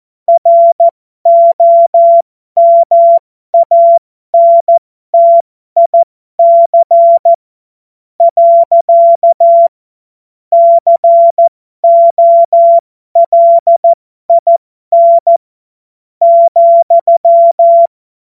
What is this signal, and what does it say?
Morse code, heard as a keyed tone, about 14 wpm, spaced as for 11 wpm, 680 Hz. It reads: ROMANTIC. COLIN,
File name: Morsekurs / Witze